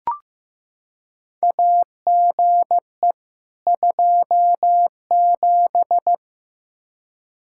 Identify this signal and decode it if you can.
Morse code, heard as a keyed tone, about 16 wpm, 700 Hz. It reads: AGE 27
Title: K1USN SST QSO Practice 15wpm